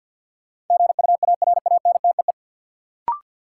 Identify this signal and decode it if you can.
Morse code, heard as a keyed tone, about 52 wpm, 700 Hz. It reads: GUARANTEE
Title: Sentences from Top 1600 Words - New Word Drill - 50wpm